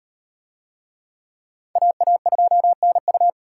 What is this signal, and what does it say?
Morse code, heard as a keyed tone, about 35 wpm, 700 Hz. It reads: AA2NU